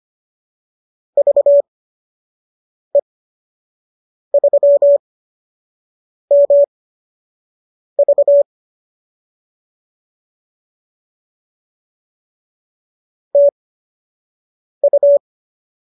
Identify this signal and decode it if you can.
Morse code, heard as a keyed tone, about 25 wpm, spaced as for 6 wpm, 570 Hz. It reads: VE3MV TU